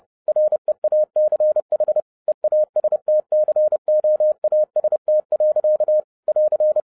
Morse code, heard as a keyed tone, about 30 wpm, 610 Hz. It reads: REACH EAST COAST. <AR>